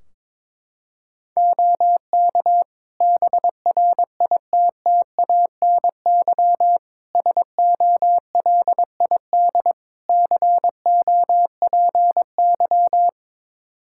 Morse code, heard as a keyed tone, about 22 wpm, 700 Hz. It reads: OK BRITTANY SOLID COPY
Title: K1USN SST QSO Practice 22wpm